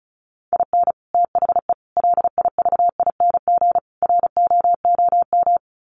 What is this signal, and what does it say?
Morse code, heard as a keyed tone, about 34 wpm, 700 Hz. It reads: IN THE LIVING ROOM
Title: Sets of 4 Words 35wpm